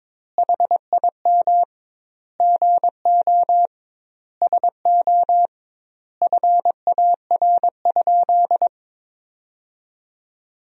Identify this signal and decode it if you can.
Morse code, heard as a keyed tone, about 22 wpm, 700 Hz. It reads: HIM GO SO FAR?
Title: 2x Word-Spacing - Sentences from Top 300 Words - 22wpm